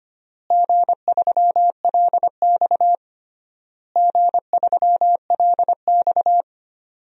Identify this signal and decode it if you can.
Morse code, heard as a keyed tone, about 25 wpm, 700 Hz. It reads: G3LX G3LX
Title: DXCC-100 Call Signs - 4 Characters - Repeated 3x - 25wpm